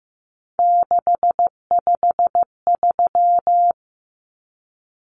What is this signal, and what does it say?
Morse code, heard as a keyed tone, about 15 wpm, 700 Hz. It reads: 653
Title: Numbers - 3-Digit 15wpm